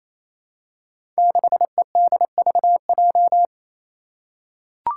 Morse code, heard as a keyed tone, about 28 wpm, 700 Hz. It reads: 6EDVJ
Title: Code Groups - 5 Letter:Numbers - 28wpm